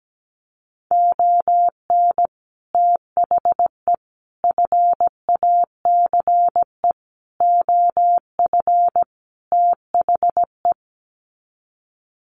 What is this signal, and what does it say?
Morse code, heard as a keyed tone, about 17 wpm, 700 Hz. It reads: ON THE FACE OF THE
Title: Sets of 5 Words Encore2 17wpm